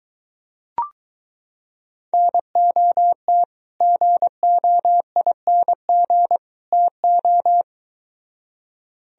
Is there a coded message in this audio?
Yes: Morse code from 2.1 to 7.6 s, heard as a keyed tone, about 24 wpm, 700 Hz. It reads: NOT GOING TO